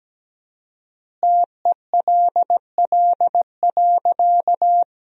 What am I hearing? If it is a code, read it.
Morse code, heard as a keyed tone, about 17 wpm, 700 Hz. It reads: TELL.
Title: 4x Word-Spacing - Sentences from Top 300 Words - 17wpm